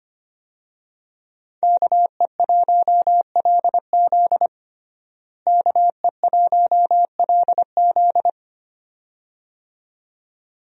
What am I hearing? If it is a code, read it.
Morse code, heard as a keyed tone, about 25 wpm, 700 Hz. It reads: KE1LZ KE1LZ
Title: USA Call Signs - 5 Characters - Repeated 2x - 25wpm